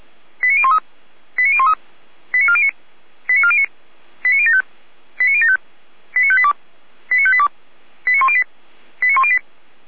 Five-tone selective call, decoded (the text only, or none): 89013 89013 89409 89409 89075 89075 89582 89582 89108 89108